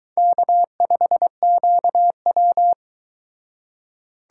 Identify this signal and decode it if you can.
Morse code, heard as a keyed tone, about 23 wpm, 700 Hz. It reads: K5QW